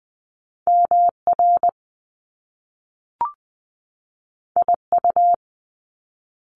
Morse code, heard as a keyed tone, about 20 wpm, 700 Hz. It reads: MR IU
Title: Top Two Letter Combinations in Words 20wpm